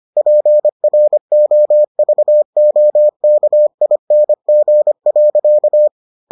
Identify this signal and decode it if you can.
Morse code, heard as a keyed tone, about 25 wpm, 600 Hz. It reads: PROVOKING.